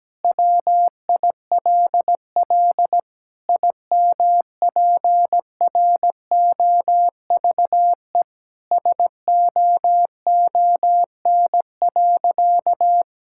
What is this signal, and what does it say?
Morse code, heard as a keyed tone, about 17 wpm, 700 Hz. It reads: WILL IMPROVE SOON.
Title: Sentences from Top 1000 Words - Review Part 5 - 17wpm